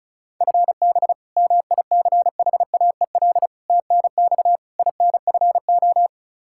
Morse code, heard as a keyed tone, about 36 wpm, 700 Hz. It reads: FB MICHAEL TNX INFO
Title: Practice QSO - Exchange 3 - More Info - 35wpm